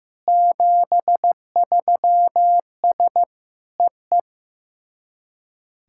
Morse code, heard as a keyed tone, about 15 wpm, 700 Hz. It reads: 73S EE
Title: SOTA QSO Practice 15wpm